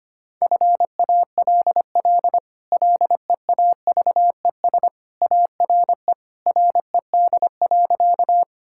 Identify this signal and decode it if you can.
Morse code, heard as a keyed tone, about 25 wpm, 700 Hz. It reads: FALL LEAVES ARE RED.
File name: Easy Sentences with Hints - 25wpm